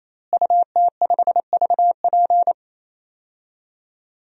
Morse code, heard as a keyed tone, about 28 wpm, 700 Hz. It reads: UT5VP
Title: DXCC-100 Call Signs - 5 Characters - Repeated 3x - 28wpm